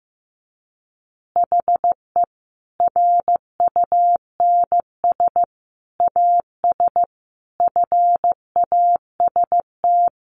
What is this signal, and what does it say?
Morse code, heard as a keyed tone, about 15 wpm, 700 Hz. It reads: HE RUNS AS FAST
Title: Additional Easy Phrases 15wpm